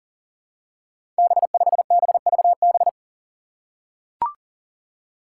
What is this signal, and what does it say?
Morse code, heard as a keyed tone, about 40 wpm, 700 Hz. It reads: B5BVB
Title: Five Challenging Letter-Number Combinations 40wpm